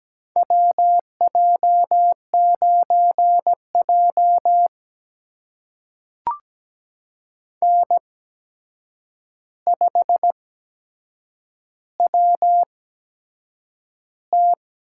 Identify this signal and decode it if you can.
Morse code, heard as a keyed tone, about 17 wpm, spaced as for 7 wpm, 700 Hz. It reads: WJ9J N5WT
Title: USA Call Signs - 4 Characters - 8x Character Spacing - 17wpm